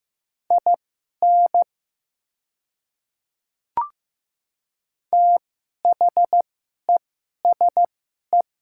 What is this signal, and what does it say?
Morse code, heard as a keyed tone, about 15 wpm, spaced as for 11 wpm, 700 Hz. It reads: IN THESE
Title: 2x Character Spacing - Top 100 Words - 15wpm